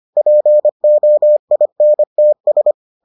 Morse code, heard as a keyed tone, about 25 wpm, 600 Hz. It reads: POINTS